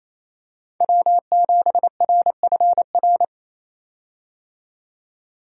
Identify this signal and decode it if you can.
Morse code, heard as a keyed tone, about 28 wpm, 700 Hz. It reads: W7RFR
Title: USA Call Signs - 5 Characters - 28wpm